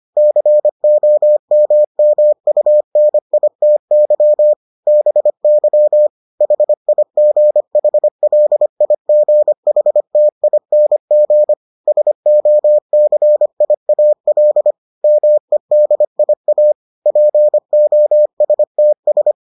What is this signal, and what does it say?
Morse code, heard as a keyed tone, about 25 wpm, 600 Hz. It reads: COMMUNITY BY HIGHLIGHTING SOCIAL MEDIA POSTS